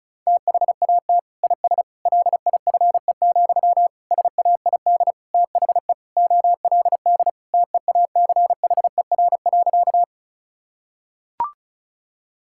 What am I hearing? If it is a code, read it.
Morse code, heard as a keyed tone, about 36 wpm, 700 Hz. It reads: THAT IS LIFE, SAID THE OLD TEACHER.
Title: Sentences from Top 1000 Words - Review Part 6 - 35wpm